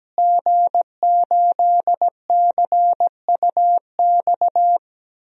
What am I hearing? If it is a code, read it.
Morse code, heard as a keyed tone, about 17 wpm, 700 Hz. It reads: G8CUX
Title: DXCC-100 Call Signs - 5 Characters - Repeated 2x - 17wpm